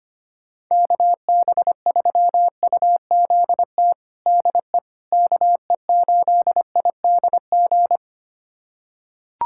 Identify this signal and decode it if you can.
Morse code, heard as a keyed tone, about 25 wpm, 700 Hz. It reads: KB3UZT DE KE8IDG
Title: USA Callsign DE Callsign 25wpm